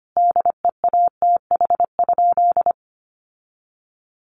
Morse code, heard as a keyed tone, about 25 wpm, 700 Hz. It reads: DEATH?
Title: Sentences from Top 1100 Words 25wpm